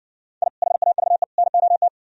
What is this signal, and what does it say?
Morse code, heard as a keyed tone, about 58 wpm, 700 Hz. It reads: I HAVE NOT